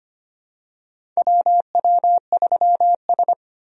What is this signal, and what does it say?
Morse code, heard as a keyed tone, about 25 wpm, 700 Hz. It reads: WW3S
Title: USA Call Signs - 4 Characters - 25wpm